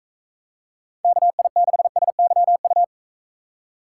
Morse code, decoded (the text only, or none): KI6SYU